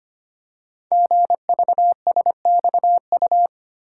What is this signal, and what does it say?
Morse code, heard as a keyed tone, about 25 wpm, 700 Hz. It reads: GVSXU
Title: Code Groups - 5 Letters - 25wpm